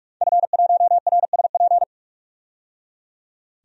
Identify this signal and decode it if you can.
Morse code, heard as a keyed tone, about 46 wpm, 700 Hz. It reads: F1RSP
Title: DXCC-100 Call Signs - 5 Characters - 45wpm